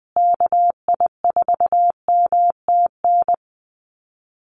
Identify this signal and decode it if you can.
Morse code, heard as a keyed tone, about 20 wpm, 700 Hz. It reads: KI4MTN
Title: USA Call Signs Repeated 2x 20wpm